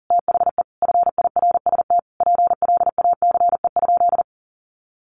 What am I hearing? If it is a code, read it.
Morse code, heard as a keyed tone, about 37 wpm, 700 Hz. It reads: THE FIRST PLACE?
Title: Sentences from Top 200 Words 40wpm